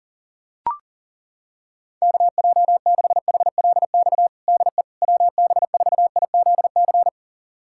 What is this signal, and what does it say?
Morse code, heard as a keyed tone, about 40 wpm, 700 Hz. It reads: KJ6HLX DE WB4IZC
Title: USA Callsign DE Callsign - 6 Characters - 40wpm